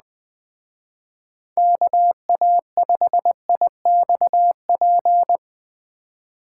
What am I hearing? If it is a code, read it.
Morse code, heard as a keyed tone, about 20 wpm, 700 Hz. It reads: KA5IXP